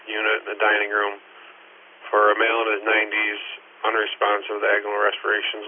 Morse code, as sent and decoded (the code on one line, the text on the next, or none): none